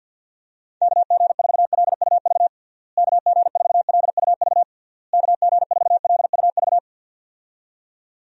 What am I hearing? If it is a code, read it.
Morse code, heard as a keyed tone, about 53 wpm, 700 Hz. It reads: KG4LUV KG4LUV KG4LUV
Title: USA Call Signs - 6 Characters - Repeated 3x - 50wpm